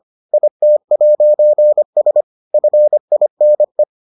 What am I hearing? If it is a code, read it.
Morse code, heard as a keyed tone, about 25 wpm, 600 Hz. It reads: IT'S FINE